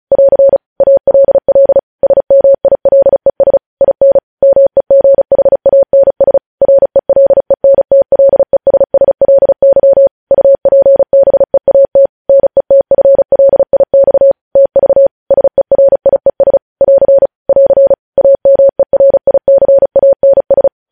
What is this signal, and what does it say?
Morse code, heard as a keyed tone, about 36 wpm, 550 Hz. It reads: <AR> ALL SMILES IN MEGHANS RELENTLESSLY UPBEAT NETFLIX TV SERIES <AR> <AR> AMERICANS
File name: podcast / News headlines-1x-35x35